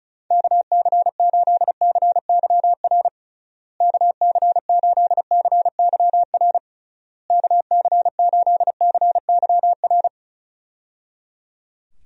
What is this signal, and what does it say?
Morse code, heard as a keyed tone, about 34 wpm, 700 Hz. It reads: KC8CYR KC8CYR KC8CYR
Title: USA Call Signs - 6 Characters - Repeated 3x - 35wpm